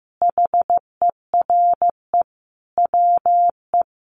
Morse code, heard as a keyed tone, about 15 wpm, 700 Hz. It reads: HERE WE
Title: Sentences from Top 1400 Words 15wpm